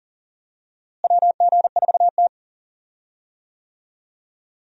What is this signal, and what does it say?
Morse code, heard as a keyed tone, about 38 wpm, 700 Hz. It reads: WG4T
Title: USA Call Signs - 4 Characters - Repeated 2x - 40wpm